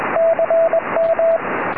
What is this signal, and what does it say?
Morse code, heard as a keyed tone, about 21 wpm, 660 Hz, with radio noise. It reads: CM
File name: tropo / ly2wr 2m